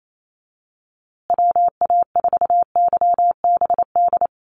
Morse code, heard as a keyed tone, about 28 wpm, 700 Hz. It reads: WA4YBD
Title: USA Call Signs Repeated 3x 28wpm